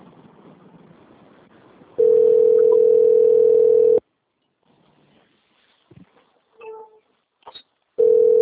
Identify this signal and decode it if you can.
Ringback tone (North America), 440 + 480 Hz.
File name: rg-1600-unknown- - - .40230